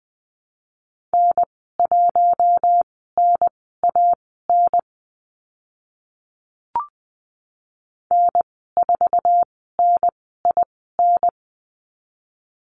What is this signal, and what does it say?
Morse code, heard as a keyed tone, about 20 wpm, spaced as for 14 wpm, 700 Hz. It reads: N1NAN N4NIN
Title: 048 Lesson 9g - Number 4 - Call Signs 20wpm